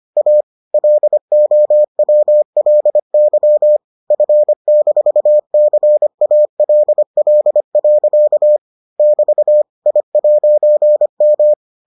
Morse code, heard as a keyed tone, about 25 wpm, 600 Hz. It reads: A LOWLY F-CALL. <BT> I'M